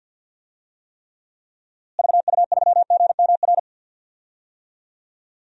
Morse code, heard as a keyed tone, about 48 wpm, 700 Hz. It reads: VU3GMR
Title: DXCC-100 Call Signs - 6 Characters - Repeated 2x - 50wpm